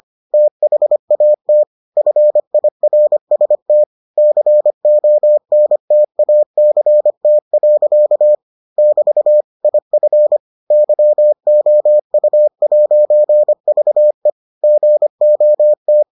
Morse code, heard as a keyed tone, about 25 wpm, 600 Hz. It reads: THAT FIRST CONTACT. <BT> IF YOU'VE GOT